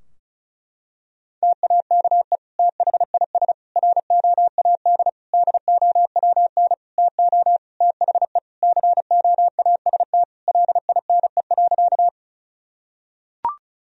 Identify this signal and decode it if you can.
Morse code, heard as a keyed tone, about 34 wpm, 700 Hz. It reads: TAKE THIS ROAD DOWN TO THE COAST LINE.
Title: Sentences from Top 1000 Words - Review Part 4 - 35wpm